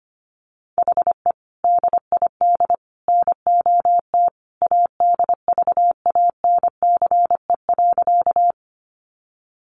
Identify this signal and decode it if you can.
Morse code, heard as a keyed tone, about 25 wpm, 700 Hz. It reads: HE DID NOT ADVANCE.
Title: Sentences from Top 1400 Words 25wpm